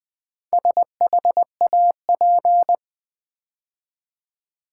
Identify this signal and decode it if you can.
Morse code, heard as a keyed tone, about 20 wpm, 700 Hz. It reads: SHAP
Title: Build A Word - 2 Letters At A Time - 6-Letter Words - 20wpm